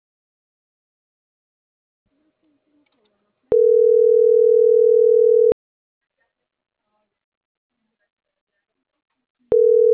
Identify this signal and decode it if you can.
Ringback tone (North America), 440 + 480 Hz.